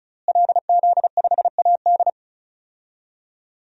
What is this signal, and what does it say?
Morse code, heard as a keyed tone, about 36 wpm, 700 Hz. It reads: LZ5AD
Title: DXCC-100 Call Signs - 5 Characters - Repeated 2x - 35wpm